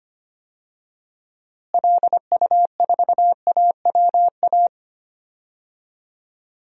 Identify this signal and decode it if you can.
Morse code, heard as a keyed tone, about 25 wpm, 700 Hz. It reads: LU4AWA
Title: DXCC-100 Call Signs - 6 Characters - Repeated 3x - 25wpm